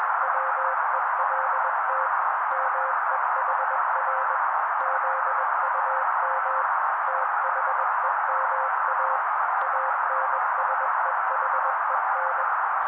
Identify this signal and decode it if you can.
Morse code, heard as a keyed tone, about 21 wpm, 530 Hz. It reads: WELT MEHR ZUM THEMA ANSEHEN
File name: daily / 20wpm noise